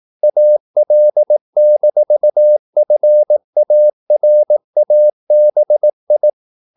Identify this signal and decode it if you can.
Morse code, heard as a keyed tone, about 18 wpm, 600 Hz. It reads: AL-FARABI
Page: Romanen Never af Ken Follet på engelsk.